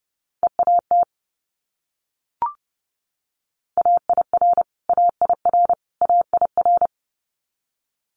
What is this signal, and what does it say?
Morse code, heard as a keyed tone, about 30 wpm, 700 Hz. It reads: EAT AIR AIR AIR
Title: Multi-letter Phonemes Repeated 3x 30wpm